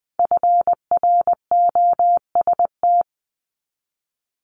Morse code, heard as a keyed tone, about 20 wpm, 700 Hz. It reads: FROST
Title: Multi-letter Phonemes Repeated 3x 20wpm